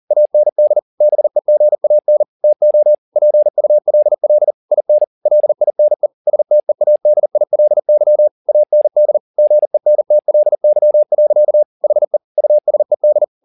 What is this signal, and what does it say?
Morse code, heard as a keyed tone, about 40 wpm, 600 Hz. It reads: AND BEGAN TO PULL IN LINE STEADILY AND GENTLY. HE USED
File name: the old man and the sea 2 40wpm